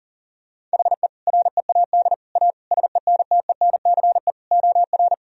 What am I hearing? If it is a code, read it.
Morse code, heard as a keyed tone, about 40 wpm, 700 Hz. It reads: HE READ A SENTENCE OR
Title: Sentences from Top 1000 Words - Review Part 1 - 40wpm